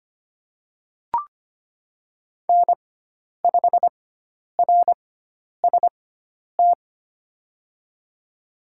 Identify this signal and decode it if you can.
Morse code, heard as a keyed tone, about 25 wpm, spaced as for 10 wpm, 700 Hz. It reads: N5RST
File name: Lesson 16g - Letter c - Call Signs 25wpm